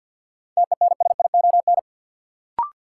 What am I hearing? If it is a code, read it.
Morse code, heard as a keyed tone, about 53 wpm, 700 Hz. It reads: TENSION
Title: Sentences from Top 1600 Words - New Word Drill Alt - 50wpm